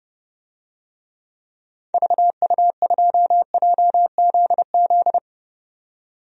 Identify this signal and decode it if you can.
Morse code, heard as a keyed tone, about 30 wpm, 700 Hz. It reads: VU2JZZ
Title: DXCC-100 Call Signs - 6 Characters - 30wpm